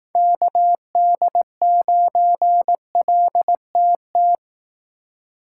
Morse code, heard as a keyed tone, about 18 wpm, 700 Hz. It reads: KD9LTT
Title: Speed Racing - USA 2x3 Call Signs 15wpm